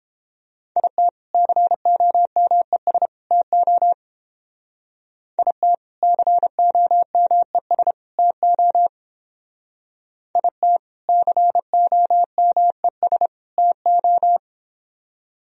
Morse code, heard as a keyed tone, about 29 wpm, 700 Hz. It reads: IT COMES TO IT COMES TO IT COMES TO